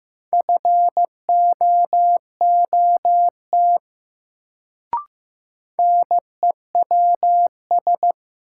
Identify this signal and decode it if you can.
Morse code, heard as a keyed tone, about 15 wpm, 700 Hz. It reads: FOOT NEWS
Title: Sets of 5 Words - New Word Drill - 15wpm